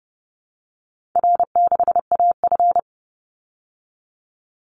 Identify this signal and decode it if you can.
Morse code, heard as a keyed tone, about 30 wpm, 700 Hz. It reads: R6AF
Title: Call Signs Repeated 2x - 30wpm